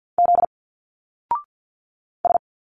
Morse code, heard as a keyed tone, about 47 wpm, 700 Hz. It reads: Z S
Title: Single Letters 50wpm